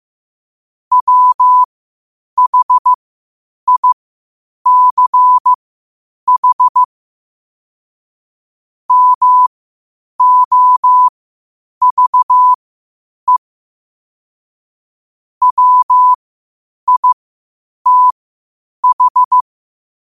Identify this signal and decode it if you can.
Morse code, heard as a keyed tone, about 15 wpm, spaced as for 8 wpm, 1000 Hz. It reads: WHICH MOVE WITH
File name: qotdfiles / 8wpm